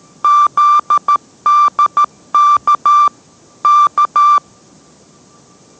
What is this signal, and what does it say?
Morse code, heard as a keyed tone, about 13 wpm, 1180 Hz. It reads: ZDK K